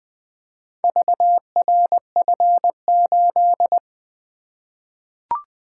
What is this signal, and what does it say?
Morse code, heard as a keyed tone, about 20 wpm, 700 Hz. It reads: VRF8
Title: Four Letter-Number Combinations - 20wpm